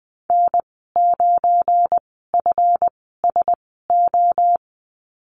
Morse code, heard as a keyed tone, about 20 wpm, spaced as for 14 wpm, 700 Hz. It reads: N9FSO
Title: 173 Lesson 28g - Number 9 - Call Signs 20wpm